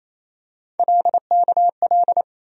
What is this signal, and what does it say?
Morse code, heard as a keyed tone, about 28 wpm, 700 Hz. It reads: LKL
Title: USA Call Sign Postfixes - 3 Letters 28wpm